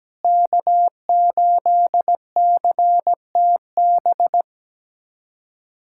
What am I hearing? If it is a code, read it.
Morse code, heard as a keyed tone, about 17 wpm, 700 Hz. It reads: K8CTB
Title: USA Callsign DE Callsign - 5 Characters - 17wpm